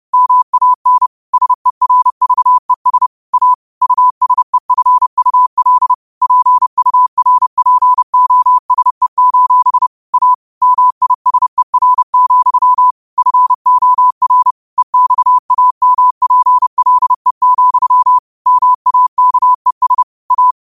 Morse code, heard as a keyed tone, about 30 wpm, 1000 Hz. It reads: MAN SERVES A USEFUL PURPOSE: A MISER, FOR EXAMPLE, MAKES A